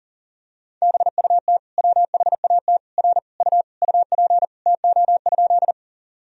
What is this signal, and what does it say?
Morse code, heard as a keyed tone, about 37 wpm, 700 Hz. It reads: BUT WHAT R U UP TO?
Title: Sentences from Top 100 Words and Abbreviations 40wpm